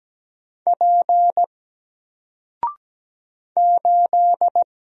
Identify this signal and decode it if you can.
Morse code, heard as a keyed tone, about 17 wpm, 700 Hz. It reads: P 8
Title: Single Letter-Number 17wpm